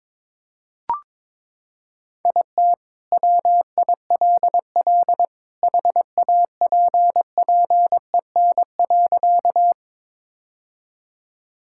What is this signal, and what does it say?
Morse code, heard as a keyed tone, about 22 wpm, 700 Hz. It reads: IT WILL HAPPEN.
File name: practice / 22wpm / Sentences with 2-Word Phrases - 22wpm